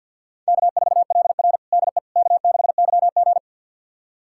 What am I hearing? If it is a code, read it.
Morse code, heard as a keyed tone, about 51 wpm, 700 Hz. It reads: K4LR DE K6YG
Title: USA Callsign DE Callsign - 4 Characters - 50wpm